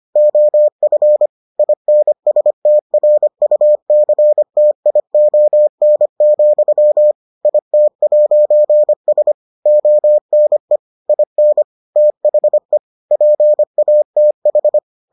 Morse code, heard as a keyed tone, about 25 wpm, 600 Hz. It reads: OF INSTRUCTION, IT'S ONE IN THE PATH